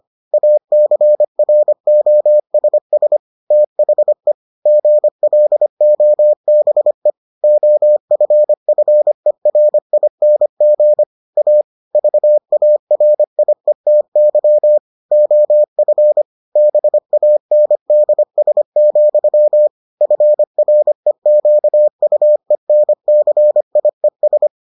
Morse code, heard as a keyed tone, about 25 wpm, 600 Hz. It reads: ACROSS THE GLOBE OFFERING A VARIETY OF BANDS, FREQUENCIES